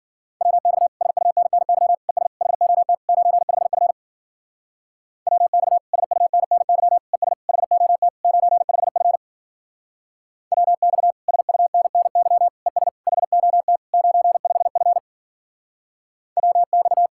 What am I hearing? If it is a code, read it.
Morse code, heard as a keyed tone, about 49 wpm, 700 Hz. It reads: WX SUNNY ES HOT 95F WX SUNNY ES HOT 95F WX SUNNY ES HOT 95F WX
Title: Speed-Racing Common QSO Phrases - 40wpm